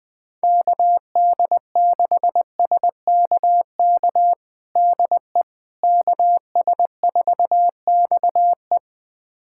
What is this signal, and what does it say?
Morse code, heard as a keyed tone, about 20 wpm, 700 Hz. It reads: KD6SKK DE KS4XE
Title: USA Callsign DE Callsign 20wpm